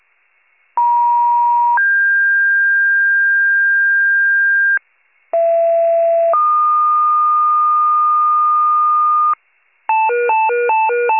Two-tone page, heard as a one-second tone, then a three-second tone, about 950 then 1600 Hz, about 670 then 1220 Hz, with noise. It is heard from 0.8 to 9.3 s.